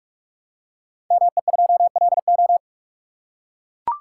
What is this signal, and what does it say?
Morse code, heard as a keyed tone, about 44 wpm, 700 Hz. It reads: ME2LO
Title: Code Groups - 5 Letter:Numbers - 45wpm